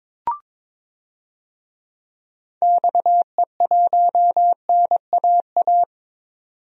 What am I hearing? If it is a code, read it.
Morse code, heard as a keyed tone, about 22 wpm, 700 Hz. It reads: XE1NAA
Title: DXCC-100 Call Signs - 6 Characters - 22wpm